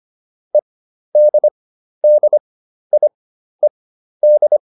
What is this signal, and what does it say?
Morse code, heard as a keyed tone, about 25 wpm, spaced as for 12 wpm, 600 Hz. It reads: EDDIED
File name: ED6 12